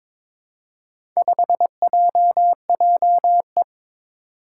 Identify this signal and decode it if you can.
Morse code, heard as a keyed tone, about 22 wpm, 700 Hz. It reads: 5JJE
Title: Four Challenging Letter-Number Combinations 22wpm